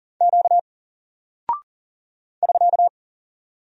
Morse code, heard as a keyed tone, about 40 wpm, 700 Hz. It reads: Q <SK>
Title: Single Character (Letter:Number:Punctuation:Prosign) Alt 40wpm